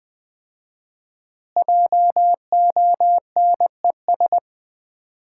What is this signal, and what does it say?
Morse code, heard as a keyed tone, about 20 wpm, 700 Hz. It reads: JONES
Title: 4x Word-Spacing - Sets of 5 Words - 20wpm